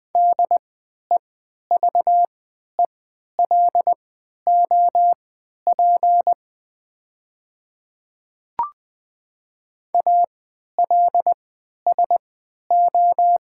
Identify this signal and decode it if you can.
Morse code, heard as a keyed tone, about 20 wpm, spaced as for 11 wpm, 700 Hz. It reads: DEVELOP ALSO
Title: 3x Character Spacing - Top 500 Words - 20wpm